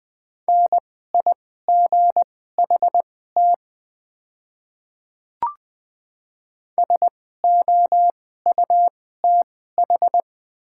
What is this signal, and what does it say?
Morse code, heard as a keyed tone, about 20 wpm, spaced as for 14 wpm, 690 Hz. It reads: NIGHT SOUTH
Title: 2x Character Spacing - Top 500 Words - 20wpm